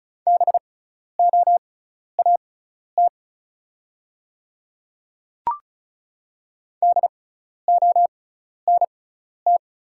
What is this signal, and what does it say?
Morse code, heard as a keyed tone, about 36 wpm, spaced as for 12 wpm, 700 Hz. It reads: BOAT DONT
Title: 6x Character Spacing - Top 500 Words - 35wpm